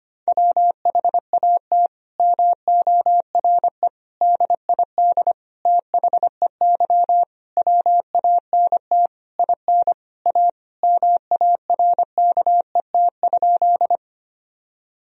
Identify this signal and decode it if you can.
Morse code, heard as a keyed tone, about 25 wpm, 700 Hz. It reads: WHAT MORE DID THEY WANT IN A MARKET?
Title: Sentences from Top 1000 Words - Review Part 3 - 25wpm